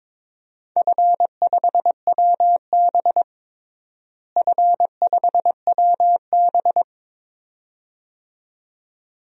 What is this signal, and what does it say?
Morse code, heard as a keyed tone, about 22 wpm, 700 Hz. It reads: F5WB F5WB
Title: DXCC-100 Call Signs - 4 Characters - Repeated 2x - 22wpm